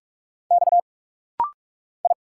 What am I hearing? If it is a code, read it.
Morse code, heard as a keyed tone, about 46 wpm, 700 Hz. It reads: X I
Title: Single Letters - Rapid-Fire 45wpm